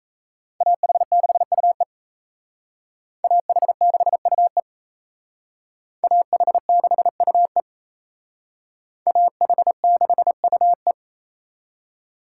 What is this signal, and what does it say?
Morse code, heard as a keyed tone, about 35 wpm, 700 Hz. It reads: AH6UE AH6UE AH6UE AH6UE
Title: Speed Racing - USA 2x2 Call Signs 28wpm